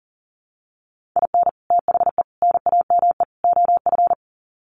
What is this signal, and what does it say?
Morse code, heard as a keyed tone, about 40 wpm, 700 Hz. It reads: IN THE NAME OF
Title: Sets of 4 Words 40wpm